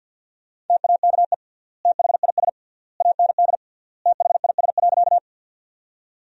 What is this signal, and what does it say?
Morse code, heard as a keyed tone, about 50 wpm, 700 Hz. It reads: TAKE THIS AND THIS.
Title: 3x Word-Spacing - Sentences from Top 200 Words - 50wpm